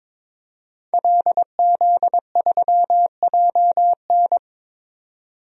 Morse code, heard as a keyed tone, about 22 wpm, 700 Hz. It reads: LZ3JN
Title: DXCC-100 Call Signs - 5 Characters - 22wpm